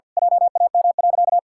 Morse code, heard as a keyed tone, about 50 wpm, 700 Hz. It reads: JAM.
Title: Pangram Challenge v2 - 50wpm